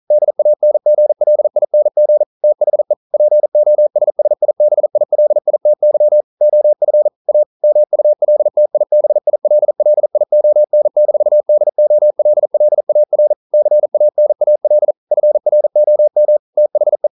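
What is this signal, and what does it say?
Morse code, heard as a keyed tone, about 42 wpm, 600 Hz. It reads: DANGLING THE POSSIBILITY OF A MULTIBILLION-DOLLAR CANAL FROM THE
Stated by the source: Morse code practice at 41 wpm, from CW med Gnister.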